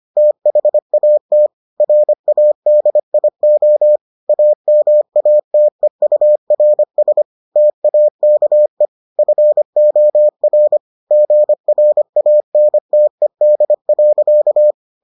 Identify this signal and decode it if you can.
Morse code, heard as a keyed tone, about 25 wpm, 600 Hz. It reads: THAT RADIO AMATEURS TAKE FOR GRANTED.